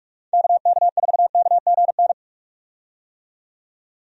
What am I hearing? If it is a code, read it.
Morse code, heard as a keyed tone, about 43 wpm, 700 Hz. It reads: KK4KGN
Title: USA Call Signs - 6 Characters - 45wpm